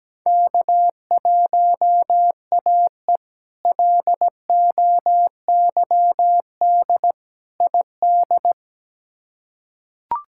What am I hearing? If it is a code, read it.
Morse code, heard as a keyed tone, about 17 wpm, 700 Hz. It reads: K1AE LOYD ID
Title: K1USN SST QSO Practice - 17wpm